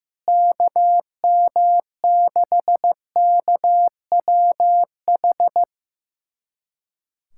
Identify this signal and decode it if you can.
Morse code, heard as a keyed tone, about 16 wpm, 700 Hz. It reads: KM6KWH
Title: USA Callsign DE Callsign 15wpm